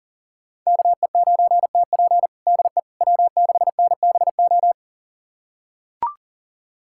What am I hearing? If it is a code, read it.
Morse code, heard as a keyed tone, about 40 wpm, 700 Hz. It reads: KE9TP DE W6NBO
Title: USA Callsign DE Callsign - 5 Characters - 40wpm